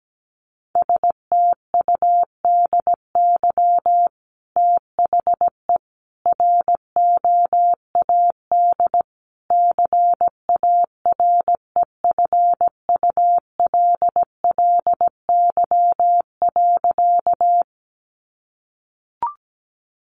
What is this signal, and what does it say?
Morse code, heard as a keyed tone, about 17 wpm, 700 Hz. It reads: STUDY THE ROAD CAREFULLY.
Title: Sentences from Top 1300 Words 17wpm